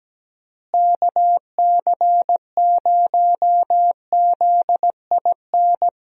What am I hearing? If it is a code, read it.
Morse code, heard as a keyed tone, about 17 wpm, 700 Hz. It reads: KC0ZIN
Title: USA Call Signs - 6 Characters - Repeated 3x - 17wpm